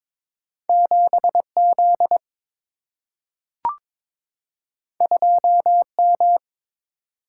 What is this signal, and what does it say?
Morse code, heard as a keyed tone, about 22 wpm, 700 Hz. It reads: 7Z 2M
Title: Single-Letter Single-Number 22wpm